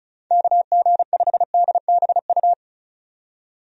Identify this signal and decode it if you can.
Morse code, heard as a keyed tone, about 36 wpm, 700 Hz. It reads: KG5DBU
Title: USA Call Signs - 6 Characters - 35wpm